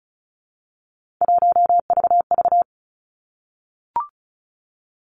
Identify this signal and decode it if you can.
Morse code, heard as a keyed tone, about 36 wpm, 700 Hz. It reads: 1VV
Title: Three Challenging Letter-Number Combinations 35wpm